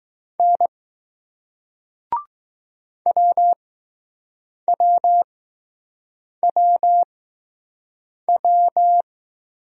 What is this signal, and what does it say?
Morse code, heard as a keyed tone, about 20 wpm, spaced as for 6 wpm, 700 Hz. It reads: N WWWW